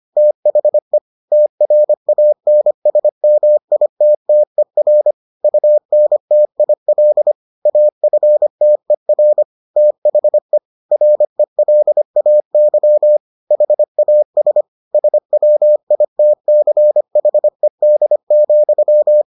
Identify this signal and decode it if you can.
Morse code, heard as a keyed tone, about 25 wpm, 600 Hz. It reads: THE TRANSMITTER UNTIL AFTER THE RELAY HAS SWITCHED,